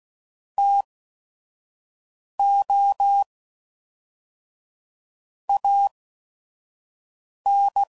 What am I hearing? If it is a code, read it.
Morse code, heard as a keyed tone, about 16 wpm, spaced as for 5 wpm, 800 Hz. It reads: TO AN
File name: CW Code Groups @5WPM / Technical Subjects / XC